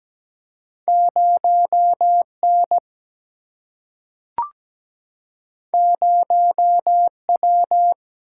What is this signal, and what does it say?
Morse code, heard as a keyed tone, about 17 wpm, 700 Hz. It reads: 0 N 0 W
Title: Single-Letter Single-Number 17wpm